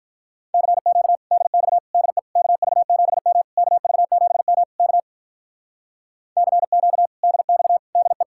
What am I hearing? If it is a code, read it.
Morse code, heard as a keyed tone, about 45 wpm, 700 Hz. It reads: CQ DX DE KV7M KV7M K CQ DX DE